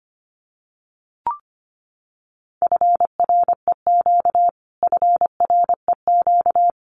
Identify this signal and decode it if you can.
Morse code, heard as a keyed tone, about 25 wpm, 700 Hz. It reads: FREQ FREQ
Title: Abbreviations Repeated 3x - 25wpm